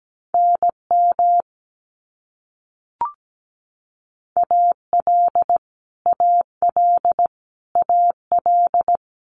Morse code, heard as a keyed tone, about 17 wpm, 700 Hz. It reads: NM AL AL AL
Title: US States Repeated 3x 17wpm